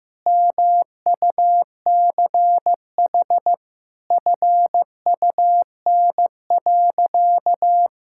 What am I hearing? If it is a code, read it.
Morse code, heard as a keyed tone, about 15 wpm, 700 Hz. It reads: MUCH FUN.
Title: Sentences from Top 1600 Words 15wpm